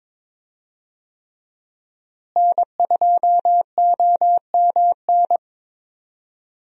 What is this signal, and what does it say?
Morse code, heard as a keyed tone, about 22 wpm, 700 Hz. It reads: N2OMN
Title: USA Call Signs - 5 Characters - Repeated 3x - 22wpm